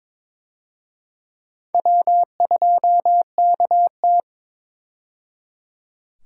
Morse code, heard as a keyed tone, about 22 wpm, 700 Hz. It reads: W2KT
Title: USA Call Signs - 4 Characters - 22wpm